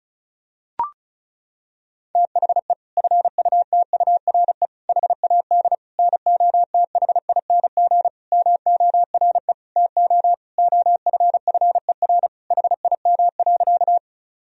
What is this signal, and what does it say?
Morse code, heard as a keyed tone, about 36 wpm, 700 Hz. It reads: THE FUTURE HAD NOTHING MORE TO OFFER HIM.
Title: Sentences from Top 1000 Words - Review Part 4 - 35wpm